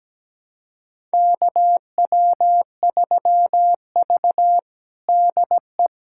Morse code, heard as a keyed tone, about 17 wpm, 700 Hz. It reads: KW3V DE
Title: USA Callsign DE Callsign - 4 Characters - 17wpm